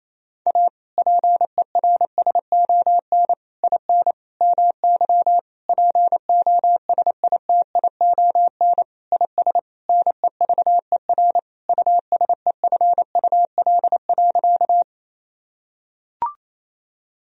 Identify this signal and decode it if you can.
Morse code, heard as a keyed tone, about 28 wpm, 700 Hz. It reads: A PERSON IN MY POSITION IS NEVER USEFUL.
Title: Sentences from Top 1000 Words - Review Part 4 - 28wpm